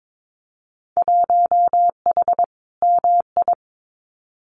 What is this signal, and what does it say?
Morse code, heard as a keyed tone, about 22 wpm, 700 Hz. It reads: 1H MI
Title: Winter Field Day Exchanges 22wpm